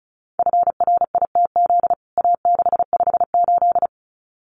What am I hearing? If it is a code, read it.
Morse code, heard as a keyed tone, about 34 wpm, 700 Hz. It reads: FRITZ A658
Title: CWT Exchange 35wpm